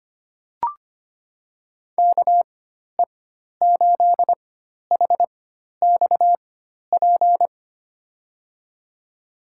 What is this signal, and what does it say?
Morse code, heard as a keyed tone, about 25 wpm, spaced as for 12 wpm, 700 Hz. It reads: KE8HXP
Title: USA Call Signs - 4x Character Spacing - 25wpm